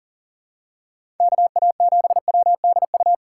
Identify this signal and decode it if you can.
Morse code, heard as a keyed tone, about 38 wpm, 700 Hz. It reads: KA7WDU
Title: USA Call Signs - 6 Characters - 2x Character Spacing - 40wpm